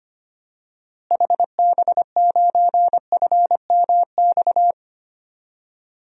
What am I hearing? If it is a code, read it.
Morse code, heard as a keyed tone, about 25 wpm, 700 Hz. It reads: HB9FMX
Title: DXCC-100 Call Signs - 6 Characters - Repeated 2x - 25wpm